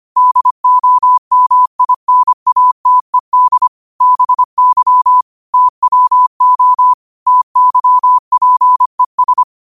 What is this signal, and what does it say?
Morse code, heard as a keyed tone, about 25 wpm, 1000 Hz. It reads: DOMINATED BY TWO TYPES
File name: qotdfiles / 25wpm